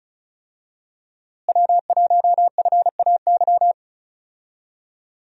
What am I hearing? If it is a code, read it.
Morse code, heard as a keyed tone, about 36 wpm, 700 Hz. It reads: W1FAY
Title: USA Call Signs - 5 Characters - Repeated 2x - 35wpm